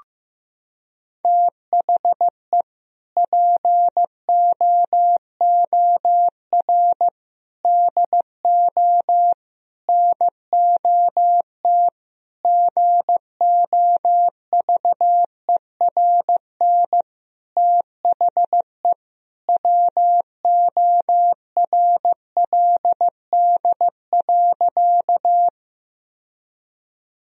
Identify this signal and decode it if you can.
Morse code, heard as a keyed tone, about 15 wpm, 700 Hz. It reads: THE POOR DO NOT GOVERN THE WORLD.
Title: Sentences from Top 1000 Words - Review Part 2 - 15wpm